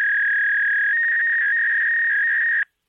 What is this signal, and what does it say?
Fax machine handshake.
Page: telephone-fax-machine